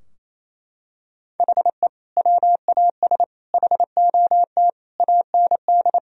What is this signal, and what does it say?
Morse code, heard as a keyed tone, about 28 wpm, 700 Hz. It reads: HE WAS HOT AND
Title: Sentences from Top 1000 Words - Review Part 3 - 28wpm